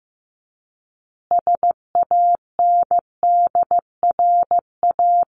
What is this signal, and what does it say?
Morse code, heard as a keyed tone, about 15 wpm, 700 Hz. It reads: SANDRA
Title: Easy Phrases 15wpm